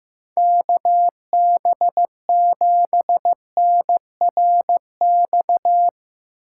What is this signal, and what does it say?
Morse code, heard as a keyed tone, about 15 wpm, 700 Hz. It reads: KB7NRX
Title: USA Callsign DE Callsign 15wpm